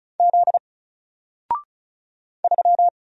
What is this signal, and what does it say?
Morse code, heard as a keyed tone, about 34 wpm, 700 Hz. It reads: Z 3
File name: Single-Letter-Number-35wpm